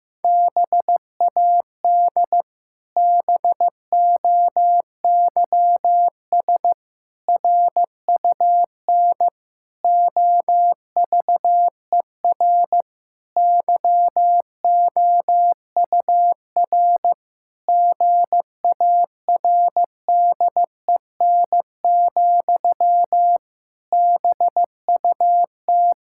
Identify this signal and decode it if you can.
Morse code, heard as a keyed tone, about 15 wpm, 700 Hz. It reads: BAD BOYS RUN OVER YOUR GARDEN, BUT
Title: Pangram Challenge v2 - 15wpm